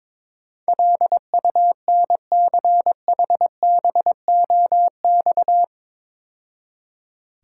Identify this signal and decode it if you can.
Morse code, heard as a keyed tone, about 22 wpm, 690 Hz. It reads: LUNCHBOX
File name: Easy Words - Speed-Racing with Hints - 22wpm